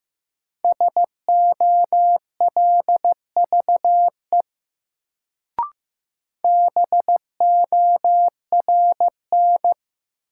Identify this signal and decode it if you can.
Morse code, heard as a keyed tone, about 15 wpm, 700 Hz. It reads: SOLVE BORN
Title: Sentences from Top 1000 Words - New Word Drill - 15wpm